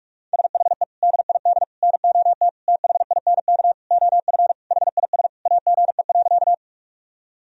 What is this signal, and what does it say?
Morse code, heard as a keyed tone, about 46 wpm, 700 Hz. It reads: SHE DID NOT THINK OF HIS AGE.
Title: Sentences from Top 1000 Words - Review Part 3 - 45wpm